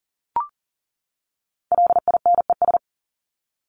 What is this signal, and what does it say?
Morse code, heard as a keyed tone, about 40 wpm, 700 Hz. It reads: LINES
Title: ES:S Suffix - Top 100 Words with Base Words 40wpm